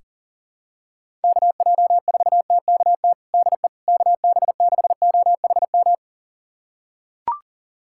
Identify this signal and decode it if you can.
Morse code, heard as a keyed tone, about 37 wpm, 700 Hz. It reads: KJ4TKT DE KB6OHM
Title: USA Callsign DE Callsign 40wpm